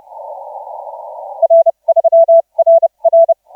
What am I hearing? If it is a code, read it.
Morse code, heard as a keyed tone, about 30 wpm, spaced as for 25 wpm, 660 Hz. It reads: R3RR
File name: 7S6W 0740UTC 17m